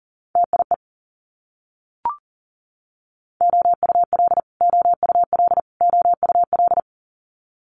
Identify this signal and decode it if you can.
Morse code, heard as a keyed tone, about 42 wpm, 700 Hz. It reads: TIE OUL OUL OUL
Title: Multi-letter Phonemes Repeated 3x 40wpm